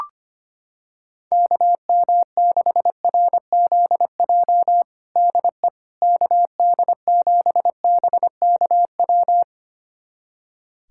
Morse code, heard as a keyed tone, about 25 wpm, 700 Hz. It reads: KM6RZJ DE KD7BKW
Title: USA Callsign DE Callsign - 6 Characters - 25wpm